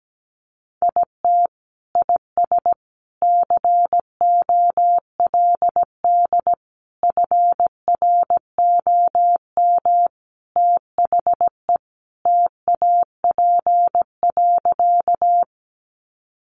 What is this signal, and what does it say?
Morse code, heard as a keyed tone, about 17 wpm, 700 Hz. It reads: IT IS COLD FROM THE TAP.
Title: Sentences from Top 1400 Words 17wpm